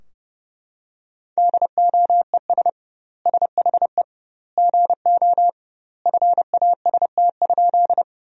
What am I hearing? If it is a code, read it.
Morse code, heard as a keyed tone, about 30 wpm, 700 Hz. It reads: DOES SHE GO FAST?
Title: 2x Word-Spacing - Sentences from Top 400 Words - 30wpm